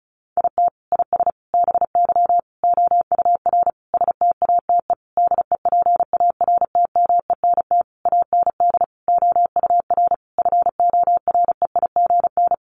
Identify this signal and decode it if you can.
Morse code, heard as a keyed tone, about 34 wpm, 700 Hz. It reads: IT IS BY OUR STATE DEPARTMENT AND OUR FOREIGN
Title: Sentences from Top 1400 Words 35wpm